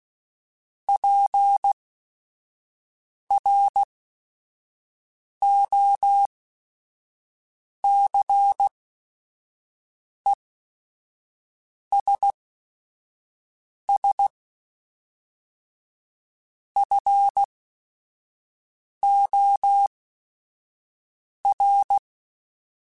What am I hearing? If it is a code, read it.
Morse code, heard as a keyed tone, about 16 wpm, spaced as for 5 wpm, 800 Hz. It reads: PROCESS FOR
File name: Technical Subjects / Technical Subjects-mp3s / AND Gate @5